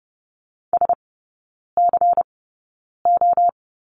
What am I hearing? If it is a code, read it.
Morse code, heard as a keyed tone, about 30 wpm, spaced as for 9 wpm, 700 Hz. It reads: SCO
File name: Lesson 18e - Letter m - Three Characters 30wpm